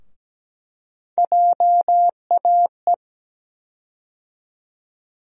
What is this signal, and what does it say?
Morse code, heard as a keyed tone, about 17 wpm, 700 Hz. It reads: JAE
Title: USA Call Sign Postfixes - 3 Letters 17wpm